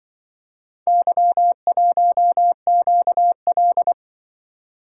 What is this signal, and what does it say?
Morse code, heard as a keyed tone, about 24 wpm, 700 Hz. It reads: Y1QL
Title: Speed-Racing - Four Letter-Number Combinations - 20wpm